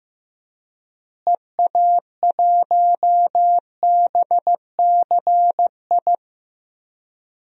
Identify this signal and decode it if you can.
Morse code, heard as a keyed tone, about 15 wpm, 700 Hz. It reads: EA1BCI
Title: DXCC-100 Call Signs - 6 Characters - Repeated 3x - 15wpm